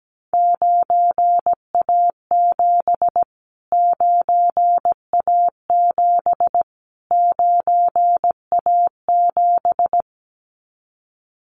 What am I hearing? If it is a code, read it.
Morse code, heard as a keyed tone, about 17 wpm, 700 Hz. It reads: 9A7 9A7 9A7
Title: Callsign Prefixes for Top 25 DXCC - Repeated 3x - 17wpm